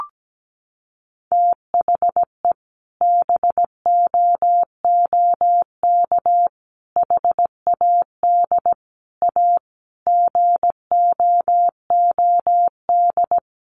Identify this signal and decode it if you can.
Morse code, heard as a keyed tone, about 17 wpm, 700 Hz. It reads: THE BOOK HAD A GOOD
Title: Sentences from Top 1400 Words 17wpm